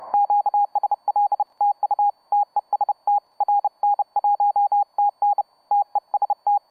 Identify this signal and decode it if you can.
Morse code, heard as a keyed tone, about 30 wpm, 820 Hz, with radio noise. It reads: QSL TU TEST RN1TN TEST
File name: Cw Rn1tn Contest 13743